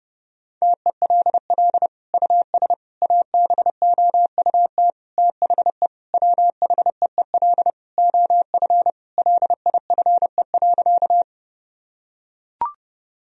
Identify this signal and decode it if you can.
Morse code, heard as a keyed tone, about 30 wpm, 700 Hz. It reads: TELL US ABOUT THE WHEEL OF LIFE.
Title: Sentences from Top 1000 Words - Review Part 3 - 30wpm